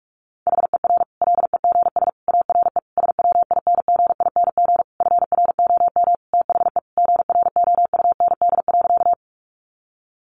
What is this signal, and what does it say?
Morse code, heard as a keyed tone, about 44 wpm, 700 Hz. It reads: HER LEGS ARE SWINGING FROM THE GROUND.
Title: Easy Phrases 45wpm